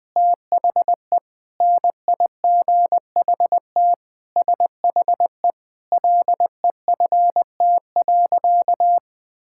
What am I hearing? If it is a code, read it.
Morse code, heard as a keyed tone, about 20 wpm, 700 Hz. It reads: THE NIGHT SHE LEFT.
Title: Sentences with 3-Word Phrases - 20wpm